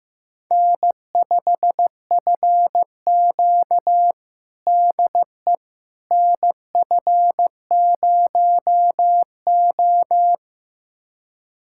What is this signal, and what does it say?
Morse code, heard as a keyed tone, about 16 wpm, 700 Hz. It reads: N5FQ DE NF0O
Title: USA Callsign DE Callsign - 4 Characters - 15wpm